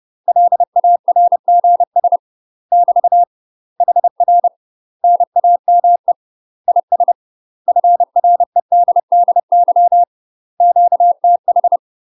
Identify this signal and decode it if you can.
Morse code, heard as a keyed tone, about 30 wpm, 700 Hz. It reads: LARGS <BT> HR NAME IS FREDDY QTH
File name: UK MASTER QSOs TRAINING / 30WPM / UK MASTER QSOs 0039